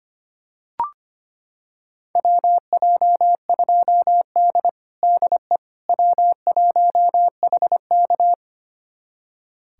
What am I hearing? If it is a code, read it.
Morse code, heard as a keyed tone, about 25 wpm, 700 Hz. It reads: WJ2D DE W1HK
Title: USA Callsign DE Callsign - 4 Characters - 25wpm